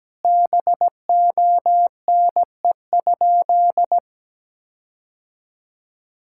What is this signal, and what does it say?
Morse code, heard as a keyed tone, about 17 wpm, 700 Hz. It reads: BONE?
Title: Sentences from Top 1600 Words 17wpm